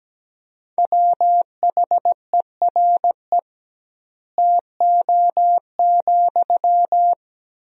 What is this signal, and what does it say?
Morse code, heard as a keyed tone, about 17 wpm, 700 Hz. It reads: WHERE TO,
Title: 2x Word-Spacing - Sentences from Top 200 Words - 17wpm